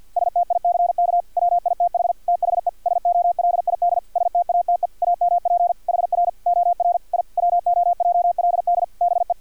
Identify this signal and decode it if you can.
Morse code, heard as a keyed tone, about 50 wpm, 700 Hz. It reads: STICK WITH THE SOLID STATE AMP HR OR I WOJLD BE
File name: archived / CRSnet